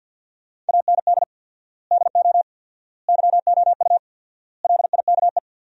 Morse code, heard as a keyed tone, about 50 wpm, 700 Hz. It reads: AND DO YOU LIKE
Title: 4x Word-Spacing - Sentences from Top 100 Words - 50wpm